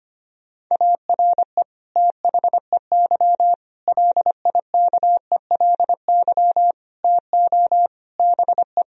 Morse code, heard as a keyed tone, about 25 wpm, 700 Hz. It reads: ARE THEY LIKELY TO BE
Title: Sentences from Top 1000 Words - Review Part 4 - 25wpm